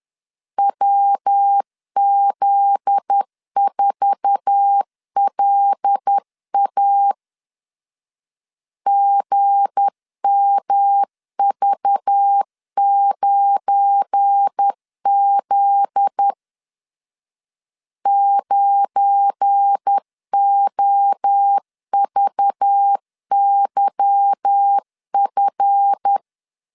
Morse code, heard as a keyed tone, about 10 wpm, 800 Hz. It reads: WZ4LA GMV9Z 9OVYF
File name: LEZIONI / 8 YJ45 / 8 YJ45 10wpm 2549